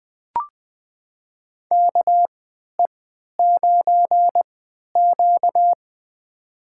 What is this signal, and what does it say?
Morse code, heard as a keyed tone, about 20 wpm, spaced as for 11 wpm, 700 Hz. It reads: KE9Q
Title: USA Call Signs - 4 Characters - 3x Character Spacing - 20wpm